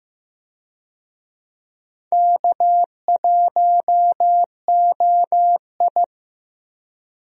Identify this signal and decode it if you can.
Morse code, heard as a keyed tone, about 15 wpm, 700 Hz. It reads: K1OI
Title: USA Call Signs - 4 Characters - 15wpm